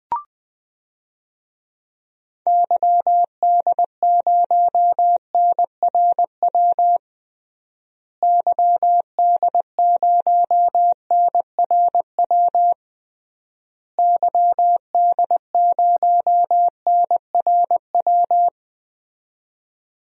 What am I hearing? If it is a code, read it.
Morse code, heard as a keyed tone, about 21 wpm, 700 Hz. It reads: YD0NRW YD0NRW YD0NRW
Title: DXCC-100 Call Signs - 6 Characters - Repeated 3x - 20wpm